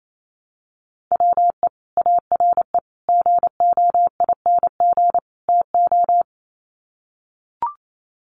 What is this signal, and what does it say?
Morse code, heard as a keyed tone, about 28 wpm, 700 Hz. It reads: WE ARE GOING TO
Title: Sets of 4 Words 28wpm